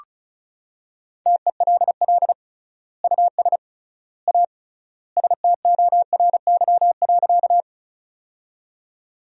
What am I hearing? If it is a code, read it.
Morse code, heard as a keyed tone, about 36 wpm, 700 Hz. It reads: TELL US A STORY.
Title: 3x Word-Spacing - Sentences from Top 300 Words - 35wpm